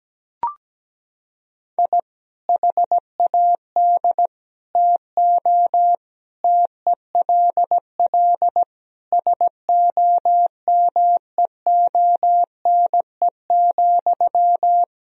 Morse code, heard as a keyed tone, about 17 wpm, 700 Hz. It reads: I HAD TO TELL SOMEONE,
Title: Sentences from Top 1000 Words - Review Part 4 - 17wpm